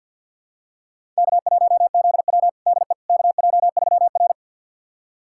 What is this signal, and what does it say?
Morse code, heard as a keyed tone, about 52 wpm, 700 Hz. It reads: K1ZW DE KJ3R
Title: USA Callsign DE Callsign - 4 Characters - 50wpm